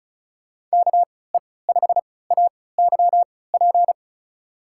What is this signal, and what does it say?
Morse code, heard as a keyed tone, about 36 wpm, spaced as for 20 wpm, 700 Hz. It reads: KE5AYP
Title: USA Call Signs - 6 Characters - 3x Character Spacing - 35wpm